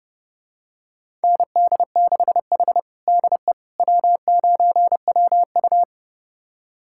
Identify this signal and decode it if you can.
Morse code, heard as a keyed tone, about 30 wpm, 700 Hz. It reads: ND6H DE W9WU
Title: USA Callsign DE Callsign - 4 Characters - 30wpm